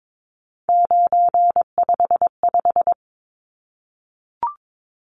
Morse code, heard as a keyed tone, about 22 wpm, 700 Hz. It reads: 955
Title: Numbers - 3-Digit 22wpm